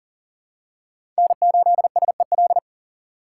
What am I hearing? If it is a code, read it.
Morse code, heard as a keyed tone, about 38 wpm, 700 Hz. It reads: N8SEL
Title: USA Call Signs - 5 Characters - 2x Character Spacing - 40wpm